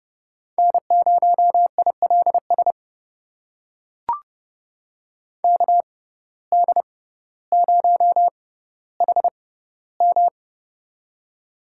Morse code, heard as a keyed tone, about 30 wpm, spaced as for 15 wpm, 700 Hz. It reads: N0ILS KD0HM
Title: USA Call Signs - 5 Characters - 6x Character Spacing - 30wpm